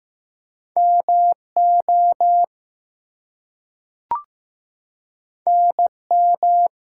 Morse code, heard as a keyed tone, about 15 wpm, 700 Hz. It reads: MO NM
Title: US States and Canadian Provinces 15wpm